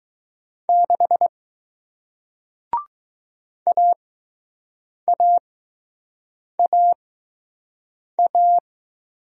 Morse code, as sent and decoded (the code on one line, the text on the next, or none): -.... / .- .- .- .-
6 AAAA